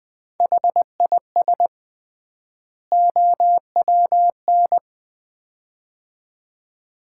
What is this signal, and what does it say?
Morse code, heard as a keyed tone, about 20 wpm, 700 Hz. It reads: HIS OWN
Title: 3x Word-Spacing - Sets of 2 Words - 20wpm